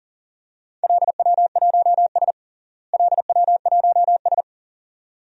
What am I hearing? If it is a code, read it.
Morse code, heard as a keyed tone, about 40 wpm, 700 Hz. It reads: LW1S LW1S
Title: DXCC-100 Call Signs - 4 Characters - Repeated 2x - 40wpm